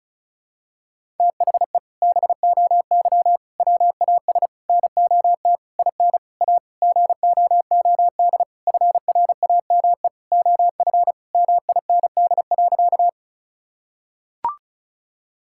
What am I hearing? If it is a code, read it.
Morse code, heard as a keyed tone, about 36 wpm, 700 Hz. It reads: THE BOY WAS NOT IN A GOOD FRAME OF MIND.
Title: Sentences from Top 1000 Words - Review Part 6 - 35wpm